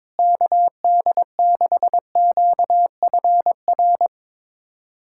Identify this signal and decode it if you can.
Morse code, heard as a keyed tone, about 22 wpm, 700 Hz. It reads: KD6QFR
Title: USA Callsign DE Callsign 22wpm